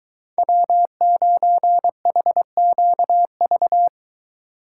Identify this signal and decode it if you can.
Morse code, heard as a keyed tone, about 22 wpm, 700 Hz. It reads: W9HQV